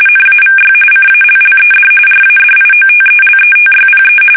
RTTY, about 45.45 baud. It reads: NAVY CALLED RADIOTELETYPE
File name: fsk850 / teletype-RTTY